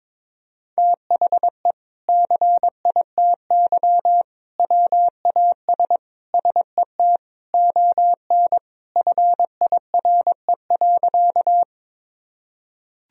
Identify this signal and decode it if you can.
Morse code, heard as a keyed tone, about 21 wpm, 700 Hz. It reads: THE CITY WAS SET ON FIRE.
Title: Sentences from Top 1000 Words - Review Part 1 - 22wpm